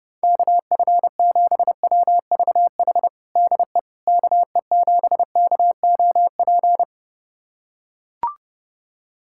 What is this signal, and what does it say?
Morse code, heard as a keyed tone, about 30 wpm, 700 Hz. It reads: KF7WVH DE KE7KOP
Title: USA Callsign DE Callsign - 6 Characters - 30wpm